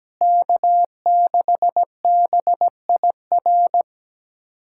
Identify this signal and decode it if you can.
Morse code, heard as a keyed tone, about 17 wpm, 700 Hz. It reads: K6BIR
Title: K1USN SST QSO Practice - 17wpm